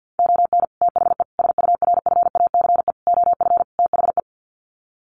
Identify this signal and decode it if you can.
Morse code, heard as a keyed tone, about 50 wpm, 700 Hz. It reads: ON THE SURFACE OF THE
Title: Sets of 5 Words Encore 50wpm